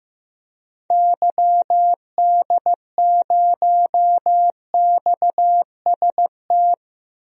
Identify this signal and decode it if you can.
Morse code, heard as a keyed tone, about 15 wpm, 700 Hz. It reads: YD0XST
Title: DXCC-100 Call Signs - 6 Characters - Repeated 3x - 15wpm